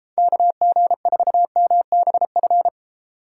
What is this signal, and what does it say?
Morse code, heard as a keyed tone, about 32 wpm, 700 Hz. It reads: KG4MBF